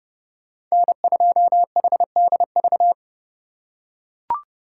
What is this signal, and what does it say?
Morse code, heard as a keyed tone, about 30 wpm, 700 Hz. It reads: N2HDV
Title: USA Call Signs - 4x Character Spacing - 30wpm